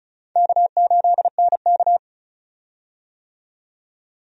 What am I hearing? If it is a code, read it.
Morse code, heard as a keyed tone, about 36 wpm, 700 Hz. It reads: K8NK
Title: USA Call Signs - 4 Characters - 35wpm